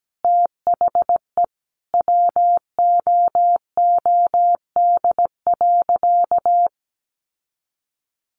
Morse code, heard as a keyed tone, about 17 wpm, 700 Hz. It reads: THE WOOD.
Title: Sentences from Top 1200 Words 17wpm